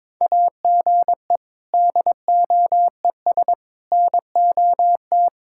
Morse code, heard as a keyed tone, about 22 wpm, 700 Hz. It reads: AGE DOES NOT
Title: Sentences from Top 1000 Words - Review Part 6 - 22wpm